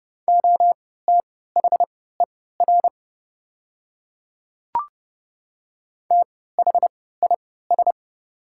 Morse code, heard as a keyed tone, about 30 wpm, spaced as for 17 wpm, 700 Hz. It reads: OTHER THIS
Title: 3x Character Spacing - Top 100 Words - 30wpm